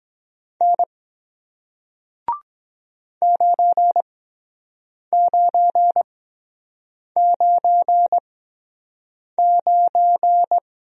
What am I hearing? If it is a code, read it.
Morse code, heard as a keyed tone, about 23 wpm, spaced as for 7 wpm, 700 Hz. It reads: N 9999